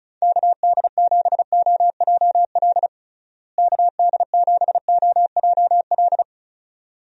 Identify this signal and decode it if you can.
Morse code, heard as a keyed tone, about 36 wpm, 700 Hz. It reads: KD7OJL KD7OJL
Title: USA Call Signs - 6 Characters - Repeated 2x - 35wpm